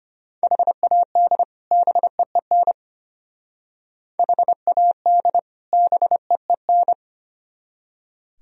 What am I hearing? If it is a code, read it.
Morse code, heard as a keyed tone, about 31 wpm, 700 Hz. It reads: HAD BEEN HAD BEEN